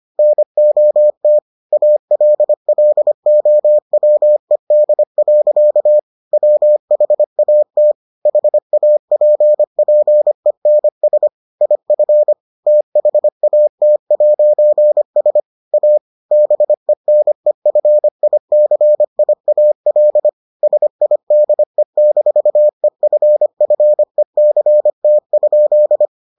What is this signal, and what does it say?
Morse code, heard as a keyed tone, about 25 wpm, 600 Hz. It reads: NOT ALLOWED. WHAT HAPPENS IF THAT'S A BENEFICIAL SIDE-EFFECT?